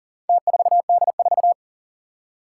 Morse code, heard as a keyed tone, about 37 wpm, 700 Hz. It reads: T4D4
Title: Four Challenging Letter-Number Combinations 40wpm